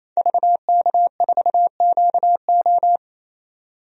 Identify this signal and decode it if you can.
Morse code, heard as a keyed tone, about 28 wpm, 700 Hz. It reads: VK4QO
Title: DXCC-100 Call Signs - 5 Characters - 28wpm